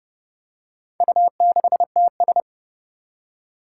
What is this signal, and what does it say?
Morse code, heard as a keyed tone, about 28 wpm, 700 Hz. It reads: U6TS
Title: Four Challenging Letter-Number Combinations 30wpm